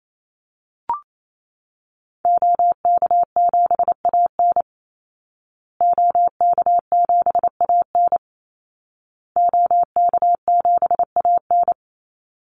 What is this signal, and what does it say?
Morse code, heard as a keyed tone, about 28 wpm, 700 Hz. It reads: OK7AN OK7AN OK7AN
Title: Call Signs Repeated 2x - 28wpm